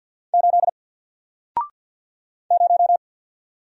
Morse code, heard as a keyed tone, about 48 wpm, 700 Hz. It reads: 8 0
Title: Single Letter-Number Alt 50wpm